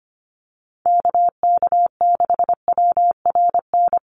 Morse code, heard as a keyed tone, about 25 wpm, 700 Hz. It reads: KK6WRN
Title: USA Call Signs Repeated 2x 25wpm